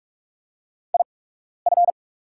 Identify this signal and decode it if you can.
Morse code, heard as a keyed tone, about 44 wpm, spaced as for 12 wpm, 700 Hz. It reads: IF
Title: 8x Character Spacing - Top 100 Words - 45wpm